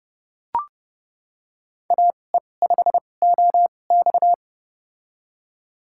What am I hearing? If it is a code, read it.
Morse code, heard as a keyed tone, about 30 wpm, spaced as for 22 wpm, 700 Hz. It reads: AE5OX
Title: USA Call Signs - 5 Characters - 2x Character Spacing - 30wpm